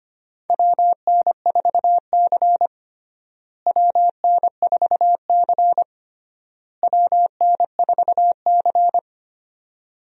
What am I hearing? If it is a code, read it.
Morse code, heard as a keyed tone, about 25 wpm, 700 Hz. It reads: WN4C WN4C WN4C
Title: USA Call Signs - 4 Characters - Repeated 3x - 25wpm